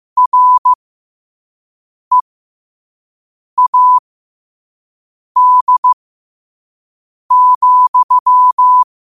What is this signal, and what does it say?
Morse code, heard as a keyed tone, about 15 wpm, spaced as for 5 wpm, 1000 Hz. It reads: READ,
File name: qotdfiles / 5wpm